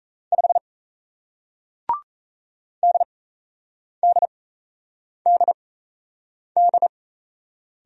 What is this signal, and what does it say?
Morse code, heard as a keyed tone, about 38 wpm, spaced as for 8 wpm, 700 Hz. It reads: 5 DDDD